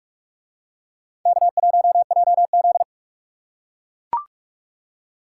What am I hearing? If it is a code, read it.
Morse code, heard as a keyed tone, about 41 wpm, 700 Hz. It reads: K1JZ